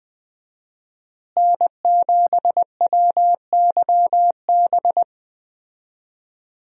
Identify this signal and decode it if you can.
Morse code, heard as a keyed tone, about 20 wpm, 700 Hz. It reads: N7WYB
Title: USA Call Signs - 5 Characters - Repeated 3x - 20wpm